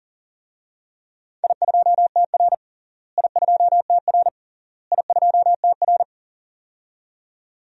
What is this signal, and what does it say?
Morse code, heard as a keyed tone, about 40 wpm, 700 Hz. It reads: I2TR I2TR I2TR
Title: DXCC-100 Call Signs - 4 Characters - Repeated 3x - 40wpm